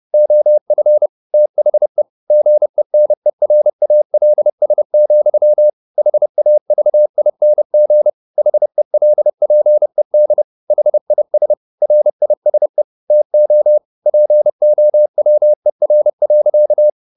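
Morse code, heard as a keyed tone, about 30 wpm, 600 Hz. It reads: OF THE GENERALS, HAVING HELPED HIS RISE TO POWER.
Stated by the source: Morse code at 30 wpm, the novel Never by Ken Follett in English, from CW med Gnister.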